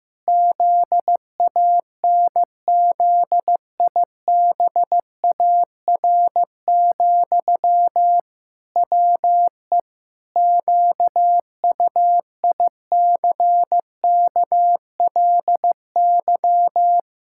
Morse code, heard as a keyed tone, about 15 wpm, 700 Hz. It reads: ZANZIBAR, WE QUICKLY
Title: Pangram Challenge v2 - 15wpm